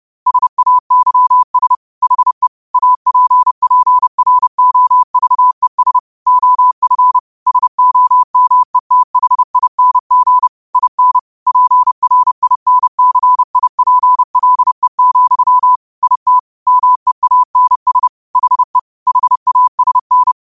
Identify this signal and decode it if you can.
Morse code, heard as a keyed tone, about 30 wpm, 1000 Hz. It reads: SAYS HE APPROVES OF SOMETHING IN PRINCIPLE, IT MEANS HE HASN